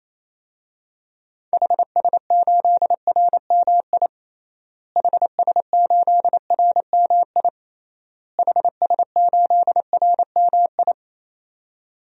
Morse code, heard as a keyed tone, about 28 wpm, 700 Hz. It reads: HS8RMI HS8RMI HS8RMI
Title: DXCC-100 Call Signs - 6 Characters - Repeated 3x - 28wpm